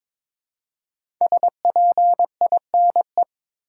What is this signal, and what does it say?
Morse code, heard as a keyed tone, about 22 wpm, 700 Hz. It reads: SPINE
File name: 5 Letter Words - First 3 Letters Given - 22wpm